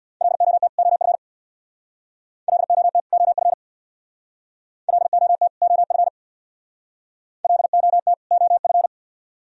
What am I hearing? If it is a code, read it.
Morse code, heard as a keyed tone, about 61 wpm, 700 Hz. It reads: LOT OF LOT OF LOT OF LOT OF
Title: Speed-Racing - Sets of 2 Words 50wpm